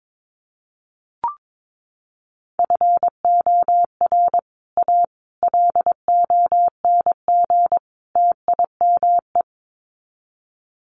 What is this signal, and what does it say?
Morse code, heard as a keyed tone, about 22 wpm, 700 Hz. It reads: FOR A LONG TIME
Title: Sets of 4 Words 22wpm